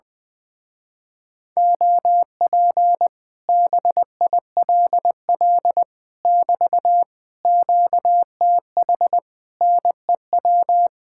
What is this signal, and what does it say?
Morse code, heard as a keyed tone, about 20 wpm, 700 Hz. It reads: OP BILL <BT> QTH NEW
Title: Speed-Racing Common QSO Phrases Encore - 20wpm